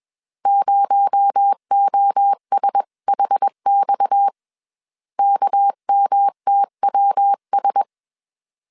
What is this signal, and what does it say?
Morse code, heard as a keyed tone, about 21 wpm, 800 Hz. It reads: 0OSHX KMTWS
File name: CW / DET / 1 6 1500C S C 20wpm 2116